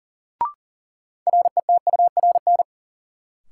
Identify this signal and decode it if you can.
Morse code, heard as a keyed tone, about 40 wpm, 700 Hz. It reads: RETURN
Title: Sentences from Top 800 Words - New Word Drill - 40wpm